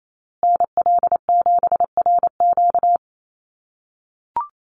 Morse code, heard as a keyed tone, about 28 wpm, 700 Hz. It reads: NL7RQ
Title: USA Call Signs Repeated 3x 28wpm